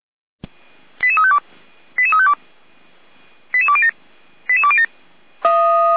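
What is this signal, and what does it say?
Five-tone selective call (ZVEI), heard as quick runs of five tones, tones of about 70 ms. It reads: 88352 88352 88207 88207
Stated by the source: ZVEI five-tone call received by SDR.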